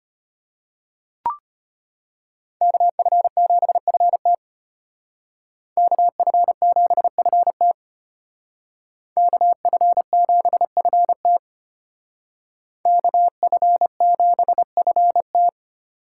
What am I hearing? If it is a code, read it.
Morse code, heard as a keyed tone, about 33 wpm, 700 Hz. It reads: KF7FT KF7FT KF7FT KF7FT